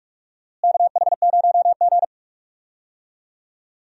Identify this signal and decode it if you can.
Morse code, heard as a keyed tone, about 47 wpm, 700 Hz. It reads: KH0G
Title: USA Call Signs - 4 Characters - Repeated 3x - 45wpm